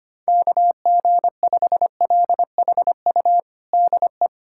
Morse code, heard as a keyed tone, about 25 wpm, 700 Hz. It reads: KG5LHU DE
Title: USA Callsign DE Callsign - 6 Characters - 25wpm